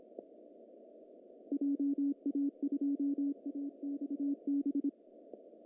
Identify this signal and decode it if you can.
Morse code, heard as a keyed tone, about 26 wpm, 280 Hz. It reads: JA2AXB
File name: CW / cqww-cw-k5tr